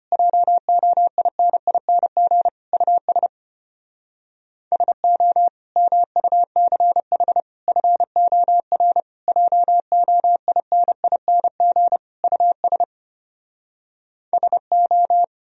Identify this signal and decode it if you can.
Morse code, heard as a keyed tone, about 33 wpm, 700 Hz. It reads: JOINING US SO MUCH FOR JOINING US SO